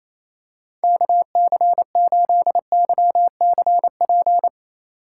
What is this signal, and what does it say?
Morse code, heard as a keyed tone, about 28 wpm, 700 Hz. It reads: KC8YCP
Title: USA Call Signs - 4x Character Spacing - 28wpm